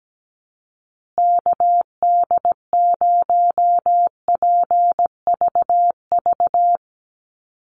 Morse code, heard as a keyed tone, about 17 wpm, 700 Hz. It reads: KD0PVV
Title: USA Call Signs Repeated 3x 17wpm